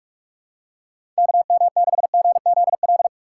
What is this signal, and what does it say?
Morse code, heard as a keyed tone, about 43 wpm, 700 Hz. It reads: KM6GZL
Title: USA Call Signs - 6 Characters - 3x Character Spacing - 45wpm